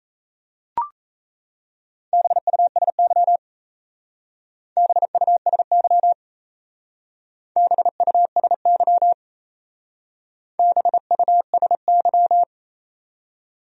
Morse code, heard as a keyed tone, about 34 wpm, 700 Hz. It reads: BUSY BUSY BUSY BUSY
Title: Speed-Racing Top 500 Words 28wpm